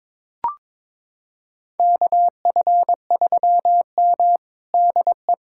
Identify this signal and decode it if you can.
Morse code, heard as a keyed tone, about 21 wpm, 700 Hz. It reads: KF3M DE
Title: USA Callsign DE Callsign - 4 Characters - 22wpm